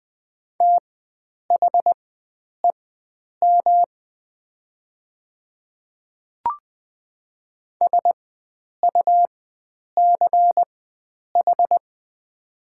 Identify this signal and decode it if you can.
Morse code, heard as a keyed tone, about 20 wpm, spaced as for 9 wpm, 700 Hz. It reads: THEM SUCH
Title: 4x Character Spacing - Top 300 Words - 20wpm